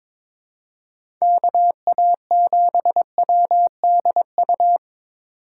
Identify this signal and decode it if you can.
Morse code, heard as a keyed tone, about 23 wpm, 700 Hz. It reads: KA7WDU
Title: USA Call Signs - 6 Characters - 4x Character Spacing - 22wpm